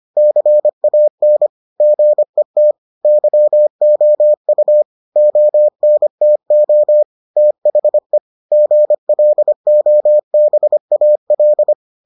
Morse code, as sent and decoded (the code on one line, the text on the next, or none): -.-. .- -. / --. - / -.-- --- ..- / --- -. - --- / - .... / --. .-.. --- -... .- .-..
CAN GET YOU ONTO THE GLOBAL